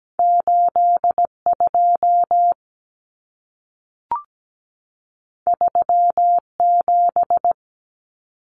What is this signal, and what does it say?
Morse code, heard as a keyed tone, about 17 wpm, 700 Hz. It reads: 82 37
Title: Numbers - 2-Digit 17wpm